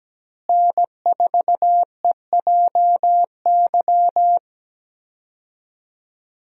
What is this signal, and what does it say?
Morse code, heard as a keyed tone, about 17 wpm, 700 Hz. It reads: N4EJY
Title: USA Call Signs - 5 Characters - Repeated 3x - 17wpm